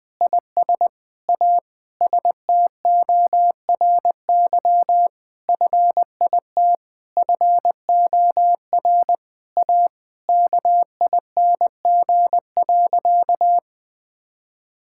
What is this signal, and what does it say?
Morse code, heard as a keyed tone, about 20 wpm, 700 Hz. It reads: IS A STORY FIT FOR A KING.
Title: Sentences from Top 1000 Words - Review Part 5 - 20wpm